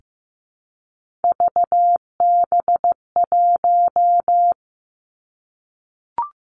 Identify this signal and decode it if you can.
Morse code, heard as a keyed tone, about 15 wpm, 700 Hz. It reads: VB1
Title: Three Challenging Letter-Number Combinations 15wpm